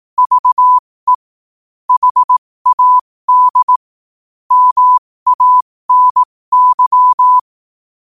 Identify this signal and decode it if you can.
Morse code, heard as a keyed tone, about 18 wpm, spaced as for 15 wpm, 1000 Hz. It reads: VE HAD MANY
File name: qotdfiles / 15wpm